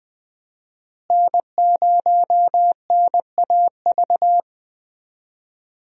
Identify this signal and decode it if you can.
Morse code, heard as a keyed tone, about 20 wpm, 700 Hz. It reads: N0NAV
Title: USA Call Signs - 5 Characters - Repeated 3x - 20wpm